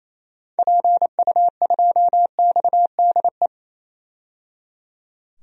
Morse code, heard as a keyed tone, about 28 wpm, 700 Hz. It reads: PU2XDE
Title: DXCC-100 Call Signs - 6 Characters - 28wpm